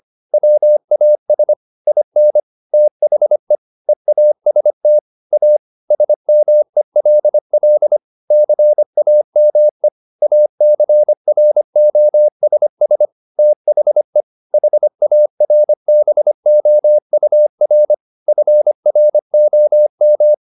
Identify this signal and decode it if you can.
Morse code, heard as a keyed tone, about 25 wpm, 600 Hz. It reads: WAS IN THE EAST A SMELL CAME ACROSS THE HARBOUR FROM
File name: the old man and the sea part 1 25wpm